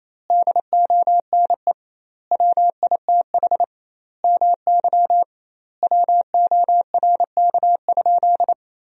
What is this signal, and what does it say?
Morse code, heard as a keyed tone, about 28 wpm, 700 Hz. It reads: DONE WITH MY WORK?
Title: 2x Word-Spacing - Sentences from Top 500 Words - 28wpm